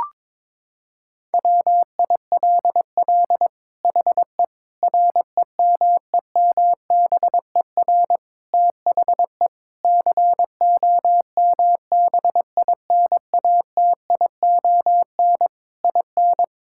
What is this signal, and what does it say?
Morse code, heard as a keyed tone, about 22 wpm, 700 Hz. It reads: WILL HE REMEMBER THE COMBINATION IN
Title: Sentences from Top 1000 Words - Review Part 6 - 22wpm